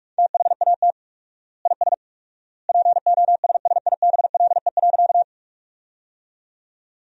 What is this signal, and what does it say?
Morse code, heard as a keyed tone, about 41 wpm, 700 Hz. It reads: THAT IS POSSIBLE.
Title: 4x Word-Spacing - Sentences from Top 500 Words - 45wpm